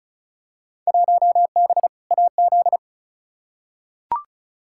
Morse code, heard as a keyed tone, about 36 wpm, 700 Hz. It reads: 1B AZ
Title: Field Day Exchanges - 35wpm